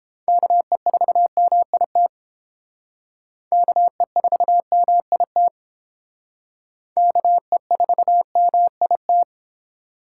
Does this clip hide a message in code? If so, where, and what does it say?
Yes: Morse code from 0.3 to 9.2 s, heard as a keyed tone, about 32 wpm, 700 Hz. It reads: KE4MIT KE4MIT KE4MIT